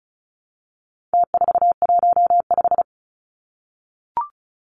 Morse code, heard as a keyed tone, about 34 wpm, 700 Hz. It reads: T415
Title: Numbers - Contest Style - 4-Digit 35wpm